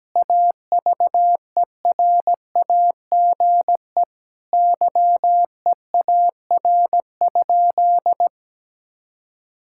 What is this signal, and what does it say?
Morse code, heard as a keyed tone, about 17 wpm, 700 Hz. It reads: AVERAGE YEAR?
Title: Sentences from Top 1000 Words - Review Part 6 - 17wpm